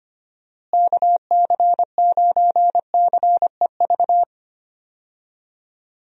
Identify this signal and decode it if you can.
Morse code, heard as a keyed tone, about 25 wpm, 700 Hz. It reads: KC9CEV
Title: USA Call Signs - 6 Characters - Repeated 2x - 25wpm